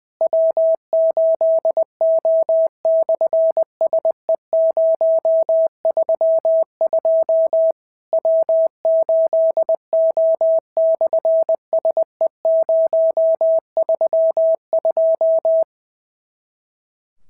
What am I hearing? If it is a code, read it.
Morse code, heard as a keyed tone, about 20 wpm, 650 Hz. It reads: W8O/SE032 W8O/SE032
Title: SOTA QSO Practice 20wpm